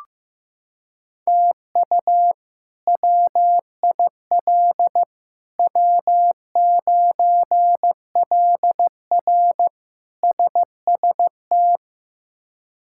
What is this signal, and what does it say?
Morse code, heard as a keyed tone, about 15 wpm, 700 Hz. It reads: TU WIL W9LR SST
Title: K1USN SST QSO Practice - 15wpm